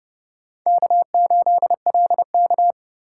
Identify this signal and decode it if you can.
Morse code, heard as a keyed tone, about 30 wpm, 700 Hz. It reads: K8LK
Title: USA Call Signs - 4 Characters - 8x Character Spacing - 30wpm